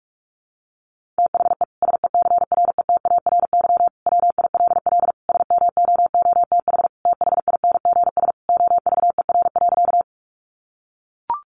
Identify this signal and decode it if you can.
Morse code, heard as a keyed tone, about 44 wpm, 700 Hz. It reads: THE SECRETARY WILL SMOOTH THINGS OVER.
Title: Sentences from Top 1400 Words 45wpm